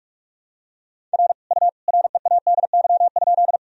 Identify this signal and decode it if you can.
Morse code, heard as a keyed tone, about 41 wpm, 700 Hz. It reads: R U READY?
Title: Sentences from Top 400 Words and Abbreviations 45wpm